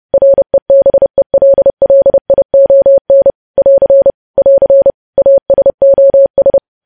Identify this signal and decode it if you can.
Morse code, heard as a keyed tone, about 30 wpm, 550 Hz. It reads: REBELLION <AR> <AR> ASOS
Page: News Headlines in Morse Code at 30 WPM